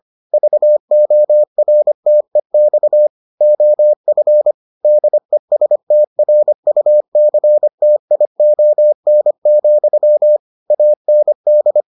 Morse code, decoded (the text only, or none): VORTEX OF DESTRUCTION, AND